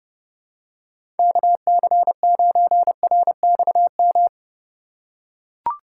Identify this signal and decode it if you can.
Morse code, heard as a keyed tone, about 32 wpm, 700 Hz. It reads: KC9RXM
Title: USA Call Signs - 6 Characters - 3x Character Spacing - 30wpm